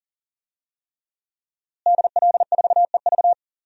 Morse code, heard as a keyed tone, about 40 wpm, 700 Hz. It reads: DL4EV
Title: DXCC-100 Call Signs - 5 Characters - Repeated 3x - 40wpm